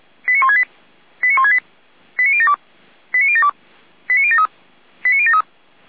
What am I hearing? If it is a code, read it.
Five-tone selective call (ZVEI), heard as quick runs of five tones, tones of about 70 ms. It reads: 89168 89168 89072 89072 89073 89073
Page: SDR/ZVEI